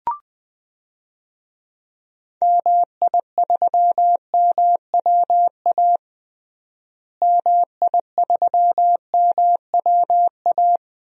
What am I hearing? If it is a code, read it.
Morse code, heard as a keyed tone, about 20 wpm, 700 Hz. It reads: MI3MWA MI3MWA
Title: DXCC-100 Call Signs - 6 Characters - Repeated 3x - 20wpm